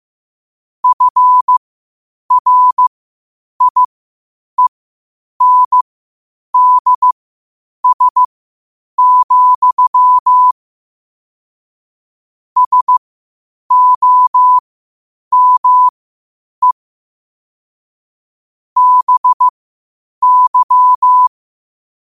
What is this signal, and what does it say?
Morse code, heard as a keyed tone, about 15 wpm, spaced as for 8 wpm, 1000 Hz. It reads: FRIENDS, SOME BY
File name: qotdfiles / 8wpm